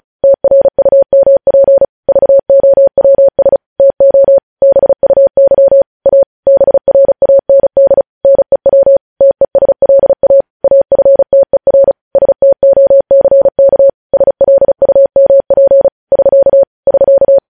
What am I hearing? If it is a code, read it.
Morse code, heard as a keyed tone, about 36 wpm, 550 Hz. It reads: TRUMP VOWS TO BUY A BRAND NEW TESLA AFTER STOCK SLUMP <SK> <SK>
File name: podcast / News headlines-1x-35x35